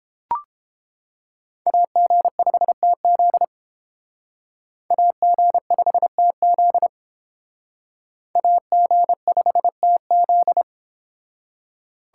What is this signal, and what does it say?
Morse code, heard as a keyed tone, about 30 wpm, 700 Hz. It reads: AG5TZ AG5TZ AG5TZ